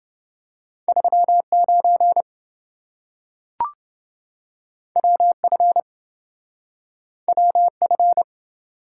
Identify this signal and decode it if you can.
Morse code, heard as a keyed tone, about 27 wpm, 700 Hz. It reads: 39 WF WF